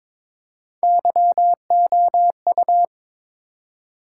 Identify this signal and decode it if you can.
Morse code, heard as a keyed tone, about 22 wpm, 700 Hz. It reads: YOU
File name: practice / 22wpm / Build A Phrase - 5-Word Phrases - 22wpm